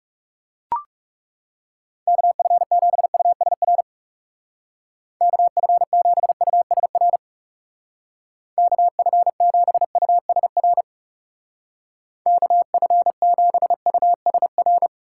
Morse code, heard as a keyed tone, about 38 wpm, 700 Hz. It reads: KF7USR KF7USR KF7USR KF7USR